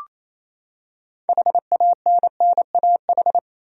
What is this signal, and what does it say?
Morse code, heard as a keyed tone, about 28 wpm, 700 Hz. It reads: HANNAH
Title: USA Common Names - 28wpm